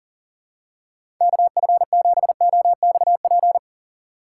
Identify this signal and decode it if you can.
Morse code, heard as a keyed tone, about 37 wpm, 700 Hz. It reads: KF7OXP
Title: USA Call Signs - 6 Characters - 3x Character Spacing - 40wpm